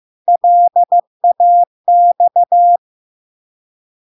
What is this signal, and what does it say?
Morse code, heard as a keyed tone, about 15 wpm, 700 Hz. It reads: LAX
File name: LARK-FD-2022-Log-in-CW-15wpm-700-Hz